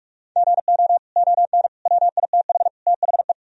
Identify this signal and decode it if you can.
Morse code, heard as a keyed tone, about 46 wpm, 700 Hz. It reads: GO ON WITH THE
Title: Sentences from Top 1000 Words - Review Part 2 - 45wpm